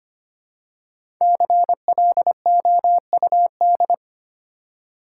Morse code, heard as a keyed tone, about 25 wpm, 700 Hz. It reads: CLOUD
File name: 5 Letter Words - First 3 Letters Given - 25wpm